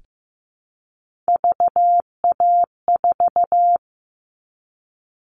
Morse code, heard as a keyed tone, about 15 wpm, 700 Hz. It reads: VA4
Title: All Cut Numbers - 3-Digit 15wpm